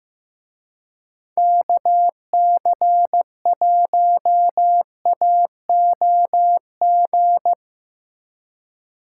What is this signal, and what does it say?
Morse code, heard as a keyed tone, about 17 wpm, 700 Hz. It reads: KC1AOG
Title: Speed Racing - USA 2x3 Call Signs 15wpm